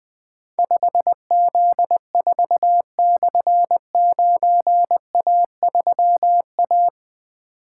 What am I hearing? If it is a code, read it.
Morse code, heard as a keyed tone, about 20 wpm, 700 Hz. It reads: 5Z4/9A3A
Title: Difficult Call Signs 20wpm